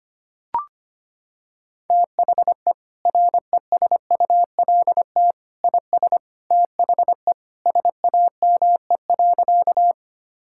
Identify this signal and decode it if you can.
Morse code, heard as a keyed tone, about 25 wpm, 700 Hz. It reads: THE RESULT IS THE SAME.
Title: Sentences from Top 1000 Words - Review Part 5 - 25wpm